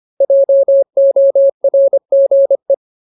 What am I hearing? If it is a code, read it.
Morse code, heard as a keyed tone, about 25 wpm, 550 Hz. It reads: JORGE
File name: Ultimate Word List / 25WPM Ultimate